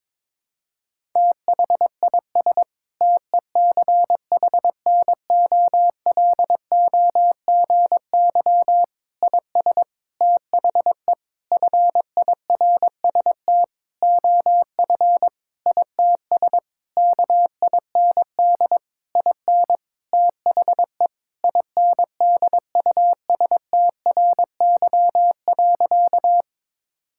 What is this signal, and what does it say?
Morse code, heard as a keyed tone, about 22 wpm, 700 Hz. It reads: THIS TECHNOLOGY IS THE FIRST OF ITS KIND IN THE INDUSTRY.
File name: practice / 22wpm / Sentences with 8-Word Phrases - 22wpm